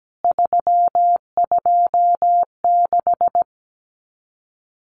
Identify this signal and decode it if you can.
Morse code, heard as a keyed tone, about 17 wpm, 700 Hz. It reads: 326
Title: CWT Exchange 17wpm